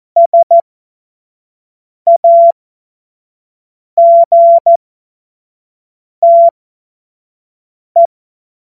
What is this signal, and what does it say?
Morse code, heard as a keyed tone, about 14 wpm, spaced as for 5 wpm, 680 Hz. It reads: SAGTE
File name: Morsekurs / Witze / Witz 26 März 2024 25 F70